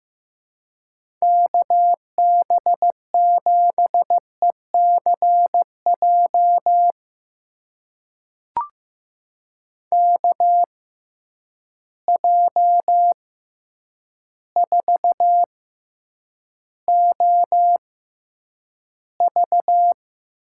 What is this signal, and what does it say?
Morse code, heard as a keyed tone, about 15 wpm, spaced as for 8 wpm, 700 Hz. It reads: KB7ECJ KJ4OV
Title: USA Call Signs - 6 Characters - 6x Character Spacing - 15wpm